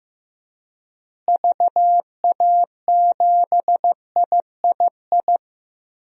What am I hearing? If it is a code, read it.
Morse code, heard as a keyed tone, about 15 wpm, 700 Hz. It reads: VA7III
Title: DXCC-100 Call Signs - 6 Characters - Repeated 3x - 15wpm